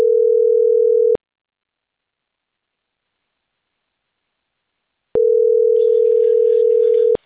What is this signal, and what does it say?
Ringback tone (North America), 440 + 480 Hz.